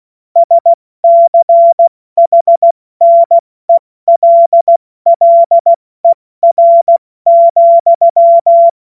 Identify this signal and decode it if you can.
Morse code, heard as a keyed tone, about 16 wpm, spaced as for 14 wpm, 680 Hz. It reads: SCHNELLER,
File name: Morsekurs / Witze / Witz 20. Januar 2026 70 F80